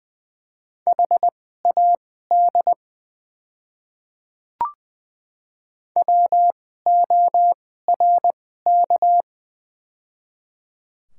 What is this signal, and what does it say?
Morse code, heard as a keyed tone, about 20 wpm, spaced as for 14 wpm, 700 Hz. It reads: HAD WORK
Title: 2x Character Spacing - Top 200 Words - 20wpm